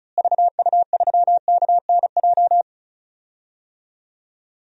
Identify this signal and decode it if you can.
Morse code, heard as a keyed tone, about 34 wpm, 700 Hz. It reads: VU3KNJ
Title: DXCC-100 Call Signs - 6 Characters - Repeated 2x - 35wpm